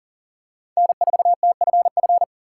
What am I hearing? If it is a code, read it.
Morse code, heard as a keyed tone, about 40 wpm, 700 Hz. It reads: N4TFF
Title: USA Call Signs - 5 Characters - 4x Character Spacing - 40wpm